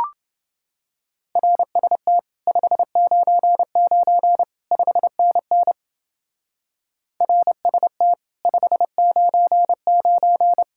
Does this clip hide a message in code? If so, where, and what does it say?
Yes: Morse code from 1.4 to 10.6 s, heard as a keyed tone, about 30 wpm, 700 Hz. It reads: RST 599 5NN RST 599